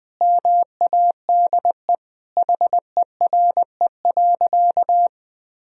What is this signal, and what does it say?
Morse code, heard as a keyed tone, about 20 wpm, 700 Hz. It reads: MADE HERE.
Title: Sentences from Top 1000 Words - Review Part 4 - 20wpm